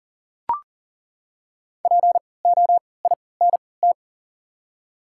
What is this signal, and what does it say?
Morse code, heard as a keyed tone, about 40 wpm, spaced as for 23 wpm, 700 Hz. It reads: POINT
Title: 3x Character Spacing - Top 200 Words - 40wpm